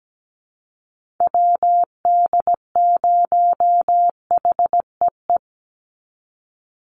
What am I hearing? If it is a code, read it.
Morse code, heard as a keyed tone, about 17 wpm, 700 Hz. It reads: WD0HEE
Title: USA Call Signs 17wpm